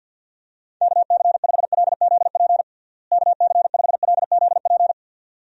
Morse code, heard as a keyed tone, about 52 wpm, 700 Hz. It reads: KC5LZP KC5LZP
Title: USA Call Signs - 6 Characters - Repeated 2x - 50wpm